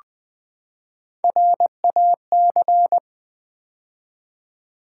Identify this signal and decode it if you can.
Morse code, heard as a keyed tone, about 20 wpm, 700 Hz. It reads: RAC
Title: Build A Word - 3 Letters At A Time - 6-Letter Words - 20wpm